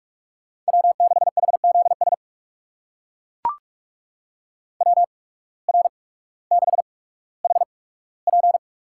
Morse code, heard as a keyed tone, about 41 wpm, spaced as for 19 wpm, 700 Hz. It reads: W6HZS WR6HP
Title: USA Call Signs - 8x Character Spacing - 45wpm